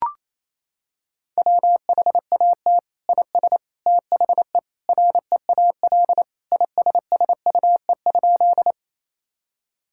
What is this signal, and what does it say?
Morse code, heard as a keyed tone, about 27 wpm, 700 Hz. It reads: WHAT IS THE REAL ISSUE?
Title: Sentences from Top 1000 Words - Review Part 6 - 28wpm